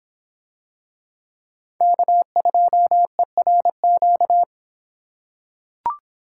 Morse code, heard as a keyed tone, about 25 wpm, 700 Hz. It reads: K2ERQ